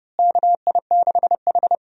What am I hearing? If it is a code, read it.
Morse code, heard as a keyed tone, about 30 wpm, 700 Hz. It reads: KI6H
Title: USA Call Signs - 4 Characters - Repeated 3x - 30wpm